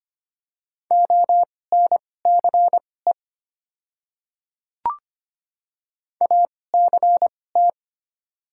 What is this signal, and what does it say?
Morse code, heard as a keyed tone, about 25 wpm, spaced as for 18 wpm, 700 Hz. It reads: ONCE ACT
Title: 2x Character Spacing - Top 500 Words - 25wpm